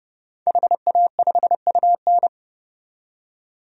Morse code, heard as a keyed tone, about 30 wpm, 700 Hz. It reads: HA5UN
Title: DXCC-100 Call Signs - 5 Characters - 30wpm